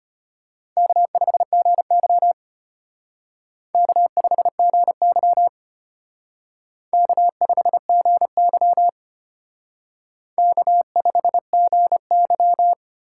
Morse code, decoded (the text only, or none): K5GY K5GY K5GY K5GY